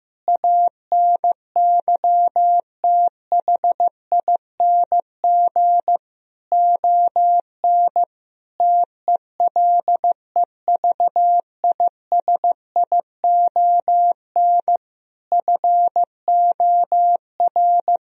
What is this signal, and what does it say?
Morse code, heard as a keyed tone, about 16 wpm, 700 Hz. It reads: ANYTHING ON TELEVISION FOR
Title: Sentences from Top 1000 Words - Review Part 4 - 15wpm